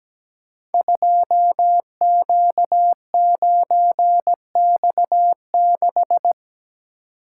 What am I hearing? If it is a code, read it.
Morse code, heard as a keyed tone, about 17 wpm, 700 Hz. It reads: 2Q9X6
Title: Code Groups - 5 Letter:Numbers - 17wpm